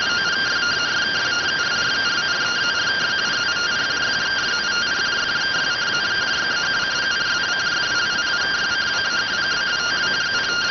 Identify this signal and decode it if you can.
Two-tone teleprinter signal, about 170 Hz shift.